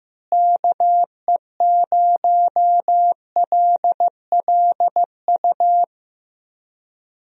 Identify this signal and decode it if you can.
Morse code, heard as a keyed tone, about 15 wpm, 700 Hz. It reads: KE0LLU
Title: USA Call Signs - 6 Characters - Repeated 3x - 15wpm